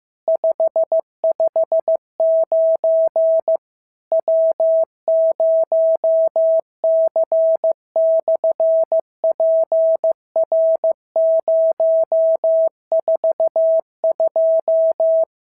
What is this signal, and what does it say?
Morse code, heard as a keyed tone, about 15 wpm, 650 Hz. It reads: 559 W0C/PR042
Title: SOTA QSO Practice 15wpm